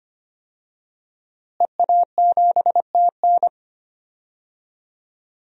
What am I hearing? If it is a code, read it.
Morse code, heard as a keyed tone, about 25 wpm, 700 Hz. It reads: EA7TN
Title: DXCC-100 Call Signs - 5 Characters - Repeated 2x - 25wpm